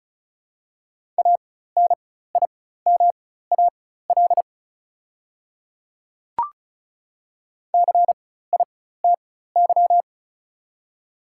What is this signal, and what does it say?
Morse code, heard as a keyed tone, about 36 wpm, spaced as for 16 wpm, 700 Hz. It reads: ANIMAL CITY
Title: 4x Character Spacing - Top 300 Words - 35wpm